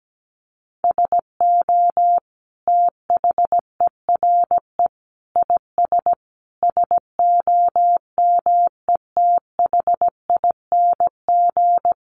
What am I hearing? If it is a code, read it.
Morse code, heard as a keyed tone, about 17 wpm, 700 Hz. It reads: SO THERE IS SOMETHING
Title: Sentences from Top 1500 Words 17wpm